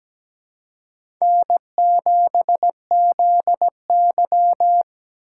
Morse code, heard as a keyed tone, about 17 wpm, 700 Hz. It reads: N7ZY
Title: USA Call Signs - 4 Characters - 2x Character Spacing - 17wpm